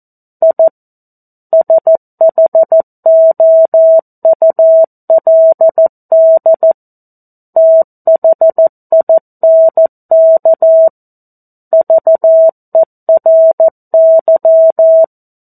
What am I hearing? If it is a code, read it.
Morse code, heard as a keyed tone, about 14 wpm, 650 Hz. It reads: I SHOULD THINK VERY
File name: 02-Chapter-02 14 WPM